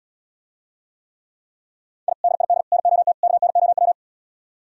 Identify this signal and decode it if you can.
Morse code, heard as a keyed tone, about 77 wpm, 700 Hz. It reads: I DID NOT KNOW